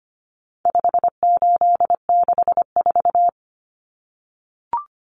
Morse code, heard as a keyed tone, about 25 wpm, 700 Hz. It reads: 5864
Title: Numbers - 4-Digit 25wpm